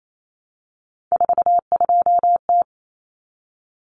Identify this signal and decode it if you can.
Morse code, heard as a keyed tone, about 28 wpm, 700 Hz. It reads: 42T
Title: Common Cut Numbers - 3-Digit 28wpm